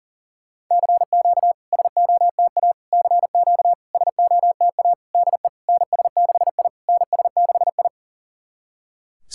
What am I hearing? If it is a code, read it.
Morse code, heard as a keyed tone, about 40 wpm, 690 Hz. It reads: CQ SOTA CQ SOTA DE NS6I NS6I
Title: SOTA QSO Practice 40wpm